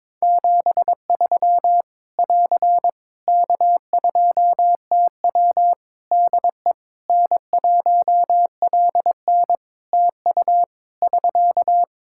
Morse code, heard as a keyed tone, about 22 wpm, 700 Hz. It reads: 73 <AR> K2TW DE N1LN TU <SK>
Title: Practice QSO - Exchange 4 - Ending - 22wpm